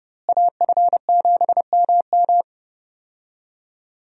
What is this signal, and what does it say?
Morse code, heard as a keyed tone, about 30 wpm, 700 Hz. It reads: AF7MM
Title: USA Call Signs - 5 Characters - Repeated 2x - 30wpm